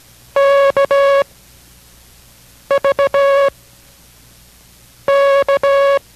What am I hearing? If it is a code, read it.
Morse code, heard as a keyed tone, about 17 wpm, 560 Hz. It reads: K V K